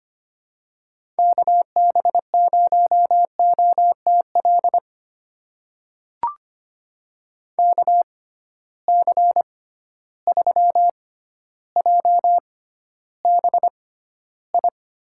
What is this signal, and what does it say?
Morse code, heard as a keyed tone, about 25 wpm, spaced as for 13 wpm, 700 Hz. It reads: KB0OTL KC3JBI
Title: USA Call Signs - 6 Characters - 6x Character Spacing - 25wpm